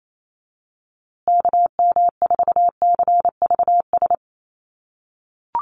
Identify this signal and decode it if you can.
Morse code, heard as a keyed tone, about 28 wpm, 690 Hz. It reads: KM4CVS
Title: USA Call Signs Repeated 3x 28wpm